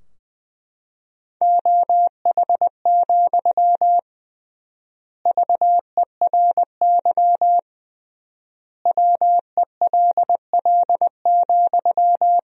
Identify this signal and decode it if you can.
Morse code, heard as a keyed tone, about 20 wpm, 700 Hz. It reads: OH, VERY WELL,
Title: 3x Word-Spacing - Sentences from Top 500 Words - 20wpm